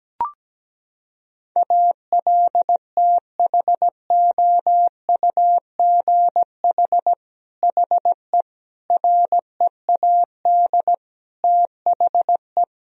Morse code, heard as a keyed tone, about 17 wpm, 700 Hz. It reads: ALTHOUGH HE READ THE
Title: Sentences from Top 1000 Words - Review Part 3 - 17wpm